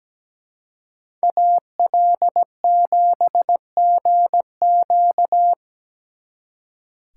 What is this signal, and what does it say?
Morse code, heard as a keyed tone, about 20 wpm, 700 Hz. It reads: AL7GQ
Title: Speed Racing - USA Call Signs 17wpm